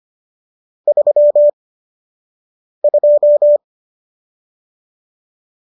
Morse code, heard as a keyed tone, about 25 wpm, spaced as for 6 wpm, 590 Hz. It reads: 32